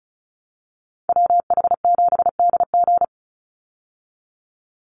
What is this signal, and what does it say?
Morse code, heard as a keyed tone, about 34 wpm, 700 Hz. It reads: WH7DG
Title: USA Call Signs 35wpm